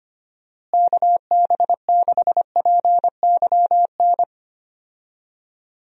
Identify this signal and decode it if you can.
Morse code, heard as a keyed tone, about 25 wpm, 700 Hz. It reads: KB6PYN
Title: USA Call Signs - 6 Characters - Repeated 2x - 25wpm